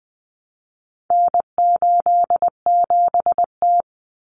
Morse code, heard as a keyed tone, about 20 wpm, 700 Hz. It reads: N87T
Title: Common Cut Numbers - 4-Digit 20wpm